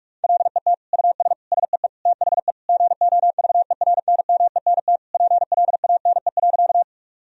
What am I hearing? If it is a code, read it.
Morse code, heard as a keyed tone, about 46 wpm, 700 Hz. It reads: LET US SEE THE GOVERNMENT PLANE.
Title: Sentences from Top 1000 Words - Review Part 3 - 45wpm